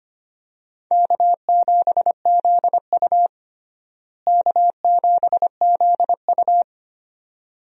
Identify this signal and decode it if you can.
Morse code, heard as a keyed tone, about 25 wpm, 700 Hz. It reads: K7ZU K7ZU
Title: USA Call Signs - 4 Characters - Repeated 2x - 25wpm